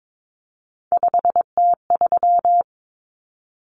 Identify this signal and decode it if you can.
Morse code, heard as a keyed tone, about 22 wpm, 700 Hz. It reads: 5T3
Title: Common Cut Numbers - 3-Digit 22wpm